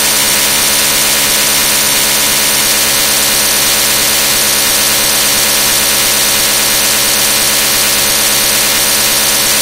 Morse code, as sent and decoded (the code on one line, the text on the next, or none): none